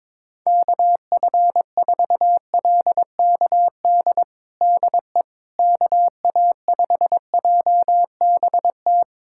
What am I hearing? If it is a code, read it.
Morse code, heard as a keyed tone, about 22 wpm, 700 Hz. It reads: KF4LKD DE KA5JBT
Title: USA Callsign DE Callsign - 6 Characters - 22wpm